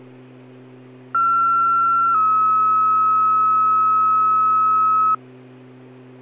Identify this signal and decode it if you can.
Two-tone page, heard as a one-second tone, then a three-second tone, about 1360 then 1290 Hz. It is heard from 1.1 to 5.2 s.